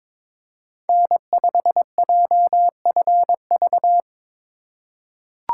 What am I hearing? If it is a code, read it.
Morse code, heard as a keyed tone, about 21 wpm, 700 Hz. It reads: N5JFV
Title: USA Call Signs - 5 Characters - 4x Character Spacing - 22wpm